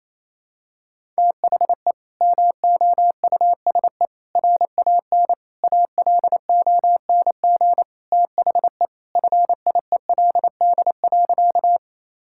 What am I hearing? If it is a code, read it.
Morse code, heard as a keyed tone, about 28 wpm, 700 Hz. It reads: THE MOUSE RAN ALONG THE FIELD.
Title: Sentences from Top 1000 Words - Review Part 5 - 28wpm